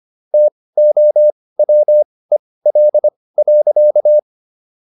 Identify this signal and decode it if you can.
Morse code, heard as a keyed tone, about 25 wpm, spaced as for 18 wpm, 600 Hz. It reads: TOWEL.